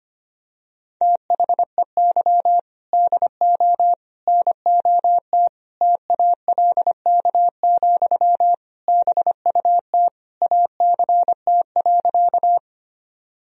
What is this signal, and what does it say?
Morse code, heard as a keyed tone, about 25 wpm, 700 Hz. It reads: THEY DO NOT TALK, BUT ACT.
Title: Sentences from Top 1000 Words - Review Part 1 - 25wpm